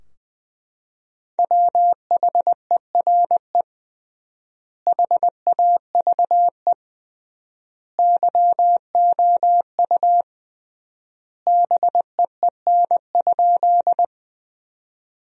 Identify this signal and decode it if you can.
Morse code, heard as a keyed tone, about 20 wpm, 700 Hz. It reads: WHERE HAVE YOU BEEN?
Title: 3x Word-Spacing - Sentences from Top 200 Words - 20wpm